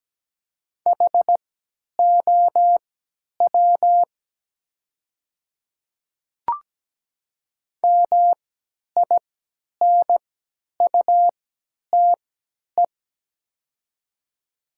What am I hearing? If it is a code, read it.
Morse code, heard as a keyed tone, about 17 wpm, spaced as for 10 wpm, 700 Hz. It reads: HOW MINUTE
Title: 3x Character Spacing - Top 500 Words - 17wpm